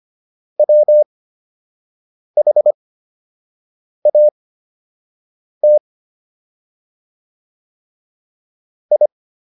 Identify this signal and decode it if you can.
Morse code, heard as a keyed tone, about 25 wpm, spaced as for 6 wpm, 600 Hz. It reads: WHAT I